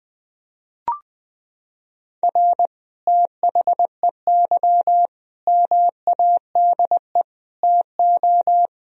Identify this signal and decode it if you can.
Morse code, heard as a keyed tone, about 20 wpm, 700 Hz. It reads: R THEY MADE TO
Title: Sentences from Top 400 Words and Abbreviations 20wpm